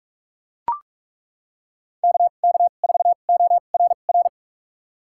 Morse code, heard as a keyed tone, about 42 wpm, spaced as for 31 wpm, 700 Hz. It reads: KK4ORR
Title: USA Call Signs - 2x Character Spacing - 45wpm